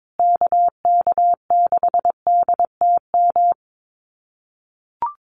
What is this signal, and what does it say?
Morse code, heard as a keyed tone, about 22 wpm, 700 Hz. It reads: KK6DTM
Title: USA Call Signs Repeated 3x 22wpm